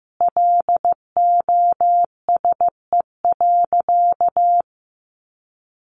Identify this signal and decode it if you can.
Morse code, heard as a keyed tone, about 15 wpm, 700 Hz. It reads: LOSE.
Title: Sentences from Top 1100 Words 15wpm